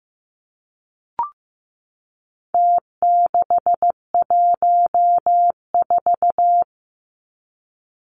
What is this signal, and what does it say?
Morse code, heard as a keyed tone, about 15 wpm, 700 Hz. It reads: T614
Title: Numbers - Contest Style - 4-Digit 15wpm